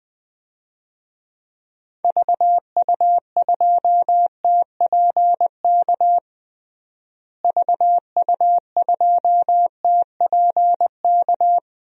Morse code, heard as a keyed tone, about 20 wpm, 700 Hz. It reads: VU2TPK VU2TPK
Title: DXCC-100 Call Signs - 6 Characters - Repeated 3x - 20wpm